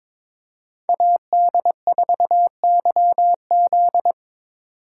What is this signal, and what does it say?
Morse code, heard as a keyed tone, about 22 wpm, 700 Hz. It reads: AD4YZ
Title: USA Call Signs - 5 Characters - 8x Character Spacing - 22wpm